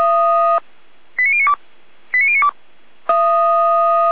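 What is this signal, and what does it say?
Five-tone selective call (ZVEI), heard as quick runs of five tones, tones of about 70 ms. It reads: 89092 89092
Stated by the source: ZVEI five-tone selective call on 455, received by SDR.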